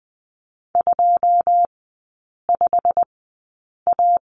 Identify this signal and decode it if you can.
Morse code, heard as a keyed tone, about 20 wpm, spaced as for 8 wpm, 700 Hz. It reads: 25A
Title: 138 Lesson 23e - Letter f - Three Characters 20wpm